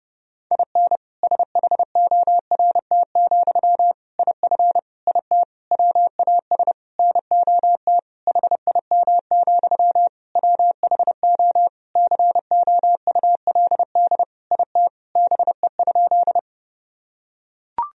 Morse code, heard as a keyed tone, about 30 wpm, 700 Hz. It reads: IN SHORT, IF IT WAS NOT HIM, WHO COULD IT BE?
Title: Sentences from Top 1000 Words - Review Part 2 - 30wpm